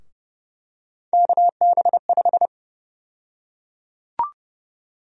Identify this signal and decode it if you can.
Morse code, heard as a keyed tone, about 30 wpm, 700 Hz. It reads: KB5
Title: USA Call Sign Prefixes - Repeated 2x - 30wpm